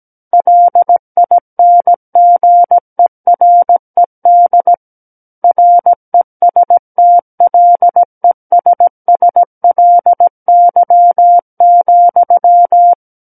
Morse code, heard as a keyed tone, about 17 wpm, 700 Hz. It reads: LINGERED RESTLESSLY,
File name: Book One / 08-Chapter 8 - 17 WPM